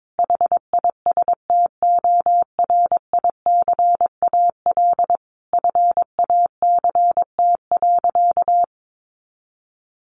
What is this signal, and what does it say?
Morse code, heard as a keyed tone, about 22 wpm, 700 Hz. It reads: HISTORICAL FACT.
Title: Sentences from Top 600 Words 22wpm